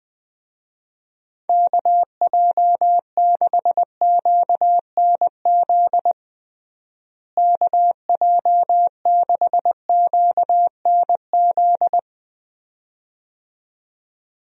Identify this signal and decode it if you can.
Morse code, heard as a keyed tone, about 21 wpm, 700 Hz. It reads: KJ6QNZ KJ6QNZ
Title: USA Call Signs - 6 Characters - Repeated 2x - 20wpm